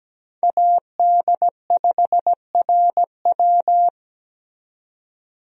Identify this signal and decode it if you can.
Morse code, heard as a keyed tone, about 17 wpm, 700 Hz. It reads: AD5RW
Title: USA Call Signs - 5 Characters - 17wpm